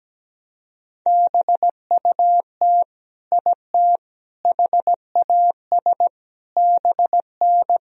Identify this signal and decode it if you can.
Morse code, heard as a keyed tone, about 17 wpm, 700 Hz. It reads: BUT IT HAS BN
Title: Sentences from Top 100 Words and Abbreviations 17wpm